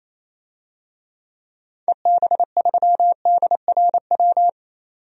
Morse code, heard as a keyed tone, about 27 wpm, 700 Hz. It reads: EB3DRW
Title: DXCC-100 Call Signs - 6 Characters - 28wpm